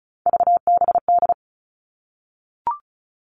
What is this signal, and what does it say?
Morse code, heard as a keyed tone, about 34 wpm, 700 Hz. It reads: VBD
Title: All Cut Numbers - 3-Digit 35wpm